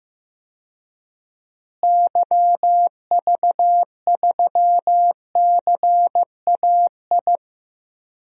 Morse code, heard as a keyed tone, about 15 wpm, 700 Hz. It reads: YV3CAI
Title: DXCC-100 Call Signs - 6 Characters - Repeated 3x - 15wpm